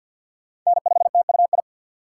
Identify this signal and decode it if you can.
Morse code, heard as a keyed tone, about 53 wpm, 700 Hz. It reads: N5TUI
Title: USA Call Signs - 4x Character Spacing - 50wpm